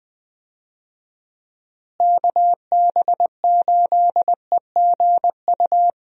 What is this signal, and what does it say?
Morse code, heard as a keyed tone, about 20 wpm, 700 Hz. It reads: KB8EGU
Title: USA Call Signs - 6 Characters - Repeated 3x - 20wpm